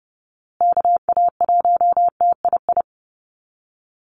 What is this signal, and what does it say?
Morse code, heard as a keyed tone, about 30 wpm, 700 Hz. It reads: KA1TII
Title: USA Call Signs Repeated 3x 30wpm